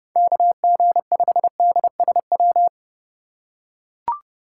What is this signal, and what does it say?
Morse code, heard as a keyed tone, about 30 wpm, 700 Hz. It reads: KG5DSW
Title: USA Call Signs - 6 Characters - 4x Character Spacing - 30wpm